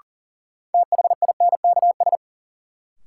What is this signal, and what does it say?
Morse code, heard as a keyed tone, about 40 wpm, 700 Hz. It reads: THINKS
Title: Sentences from Top 1000 Words - Review Part 6 - New Word Drill - 40wpm